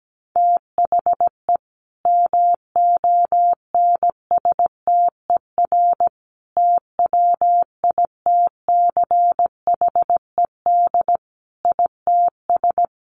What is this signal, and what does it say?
Morse code, heard as a keyed tone, about 17 wpm, 700 Hz. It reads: THE MONSTER TWITCHED ITS
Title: A Sound Of Thunder 17wpm